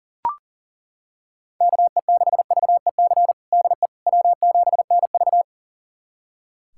Morse code, heard as a keyed tone, about 40 wpm, 700 Hz. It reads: KE6VEC DE W7NV
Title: USA Callsign DE Callsign 40wpm